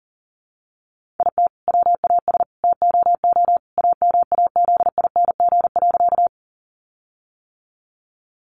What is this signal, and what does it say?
Morse code, heard as a keyed tone, about 40 wpm, 700 Hz. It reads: IT WAS TOO AMAZING.
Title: Sentences from Top 1500 Words 40wpm